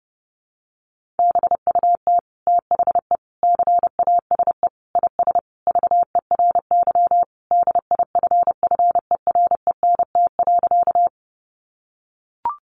Morse code, heard as a keyed tone, about 30 wpm, 700 Hz. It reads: BUT THE CASE IS VERY DIFFERENT.
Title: Sentences from Top 700 Words 30wpm